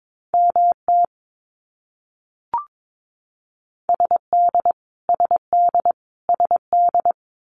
Morse code, heard as a keyed tone, about 22 wpm, 700 Hz. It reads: MT SD SD SD
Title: US States Repeated 3x 22wpm